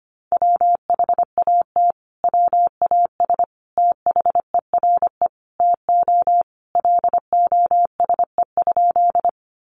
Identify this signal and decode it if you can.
Morse code, heard as a keyed tone, about 25 wpm, 700 Hz. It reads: WHAT WAS THERE TO LOSE?
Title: Sentences from Top 1100 Words 25wpm